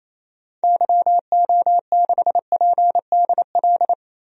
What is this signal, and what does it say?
Morse code, heard as a keyed tone, about 28 wpm, 700 Hz. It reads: YO6PDL
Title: DXCC-100 Call Signs - 6 Characters - Repeated 2x - 28wpm